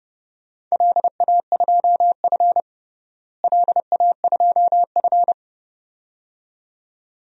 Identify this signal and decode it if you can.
Morse code, heard as a keyed tone, about 30 wpm, 700 Hz. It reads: LA2F LA2F
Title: DXCC-100 Call Signs - 4 Characters - Repeated 2x - 30wpm